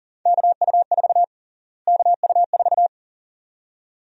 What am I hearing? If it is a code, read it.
Morse code, heard as a keyed tone, about 37 wpm, 700 Hz. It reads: KU4 KU4
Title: USA Call Sign Prefixes - Repeated 2x - 40wpm